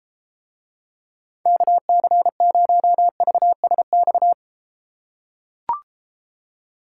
Morse code, heard as a keyed tone, about 33 wpm, 700 Hz. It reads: KC0VSX